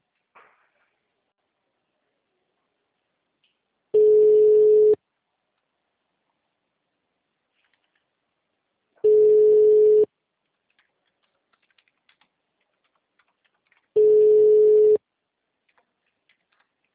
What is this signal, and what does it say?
Ringback tone (Europe, 425 Hz).